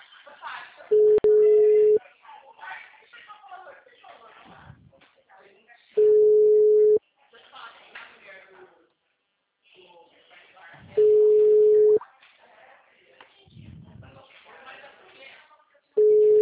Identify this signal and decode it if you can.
Ringback tone (Europe, 425 Hz), with speech.